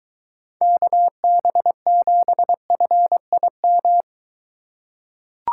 Morse code, heard as a keyed tone, about 23 wpm, 700 Hz. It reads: KB7FIM